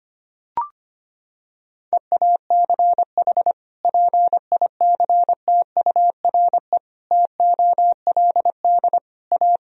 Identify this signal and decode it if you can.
Morse code, heard as a keyed tone, about 25 wpm, 700 Hz. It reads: EACH PICTURE TOLD A
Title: Sentences from Top 1000 Words - Review Part 1 - 25wpm